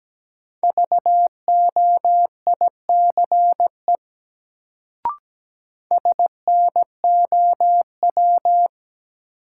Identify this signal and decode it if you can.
Morse code, heard as a keyed tone, about 17 wpm, 700 Hz. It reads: VOICE SNOW
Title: Sentences from Top 500 Words - New Word Drill - 17wpm